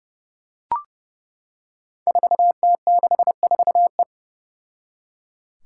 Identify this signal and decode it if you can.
Morse code, heard as a keyed tone, about 30 wpm, 700 Hz. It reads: 4T64E
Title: Five Challenging Letter-Number Combinations 30wpm